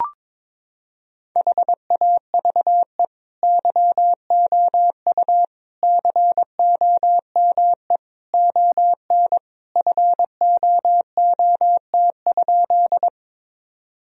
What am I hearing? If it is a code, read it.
Morse code, heard as a keyed tone, about 22 wpm, 700 Hz. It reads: HAVE YOU COME ON FOOT?
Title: Sentences from Top 1000 Words - Review Part 3 - 22wpm